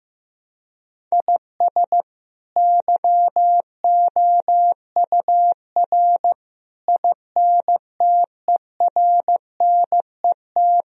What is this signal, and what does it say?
Morse code, heard as a keyed tone, about 15 wpm, 700 Hz. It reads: IS YOUR INTERNET
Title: Sentences from Top 1000 Words - Review Part 4 - 15wpm